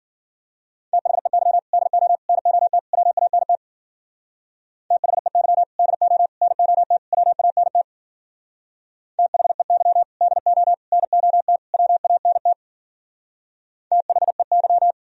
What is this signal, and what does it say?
Morse code, heard as a keyed tone, about 52 wpm, 700 Hz. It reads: THEY DO NOT WANT THEY DO NOT WANT THEY DO NOT WANT THEY